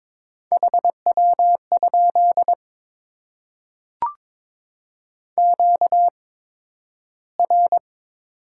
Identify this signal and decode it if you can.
Morse code, heard as a keyed tone, about 22 wpm, spaced as for 12 wpm, 700 Hz. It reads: HW? QR
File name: Top 20 QSO Elements - 8x Character Spacing - 22wpm